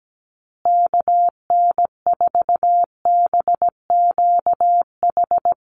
Morse code, heard as a keyed tone, about 17 wpm, 700 Hz. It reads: KN4BQH
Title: USA Call Signs Repeated 2x 17wpm